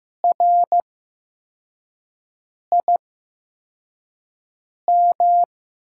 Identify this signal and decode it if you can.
Morse code, heard as a keyed tone, about 15 wpm, spaced as for 4 wpm, 700 Hz. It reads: RIM
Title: USA Call Signs - 8x Character Spacing - 15wpm